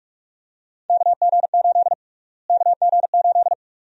Morse code, heard as a keyed tone, about 46 wpm, 700 Hz. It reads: KG8 KG8
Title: USA Call Sign Prefixes - Repeated 2x - 45wpm